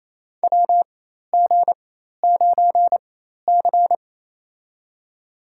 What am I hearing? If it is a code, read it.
Morse code, heard as a keyed tone, about 28 wpm, spaced as for 13 wpm, 700 Hz. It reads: WG9C
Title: USA Call Signs - 4 Characters - 4x Character Spacing - 28wpm